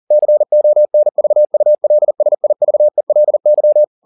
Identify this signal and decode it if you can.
Morse code, heard as a keyed tone, about 40 wpm, 600 Hz. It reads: CONVULSIVELY
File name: the old man and the sea 2 40wpm